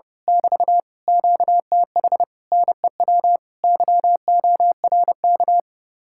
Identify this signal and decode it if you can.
Morse code, heard as a keyed tone, about 28 wpm, 700 Hz. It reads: <BT> QTH NEW YORK
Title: Speed-Racing Common QSO Phrases Encore - 30wpm